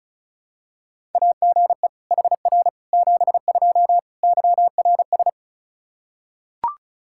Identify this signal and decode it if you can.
Morse code, heard as a keyed tone, about 34 wpm, 700 Hz. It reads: AGE HR 72 YRS
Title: K1USN SST QSO Practice 35wpm